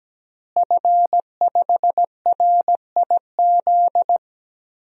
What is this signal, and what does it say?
Morse code, heard as a keyed tone, about 17 wpm, 700 Hz. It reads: F5RIZ
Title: DXCC-100 Call Signs - 5 Characters - Repeated 3x - 17wpm